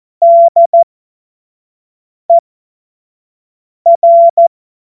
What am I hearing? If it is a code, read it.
Morse code, heard as a keyed tone, about 14 wpm, spaced as for 5 wpm, 680 Hz. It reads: DER
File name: Morsekurs / Witze / Witz 07. März 2023 25 F70